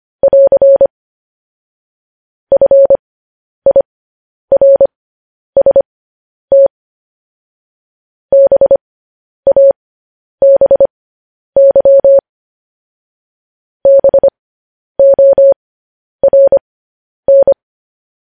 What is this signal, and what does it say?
Morse code, heard as a keyed tone, about 25 wpm, spaced as for 10 wpm, 550 Hz. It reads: <AR> FIRST BABY BORN
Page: news headlines in morse code at 10 wpm